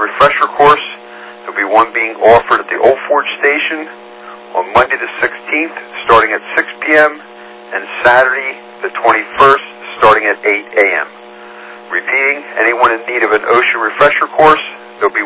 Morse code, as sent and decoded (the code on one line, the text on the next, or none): none